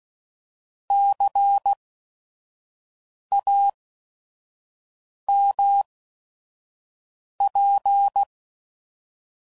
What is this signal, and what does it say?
Morse code, heard as a keyed tone, about 16 wpm, spaced as for 5 wpm, 800 Hz. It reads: CAMP
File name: Prefixes / By First Letter / Cs